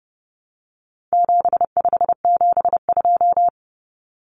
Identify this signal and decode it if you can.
Morse code, heard as a keyed tone, about 30 wpm, 700 Hz. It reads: 7572
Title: Numbers - 4-Digit 30wpm